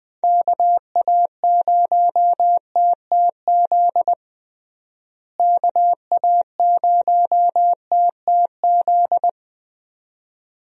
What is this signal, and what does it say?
Morse code, heard as a keyed tone, about 20 wpm, 700 Hz. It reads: KA0TTZ KA0TTZ
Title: USA Call Signs - 6 Characters - Repeated 2x - 20wpm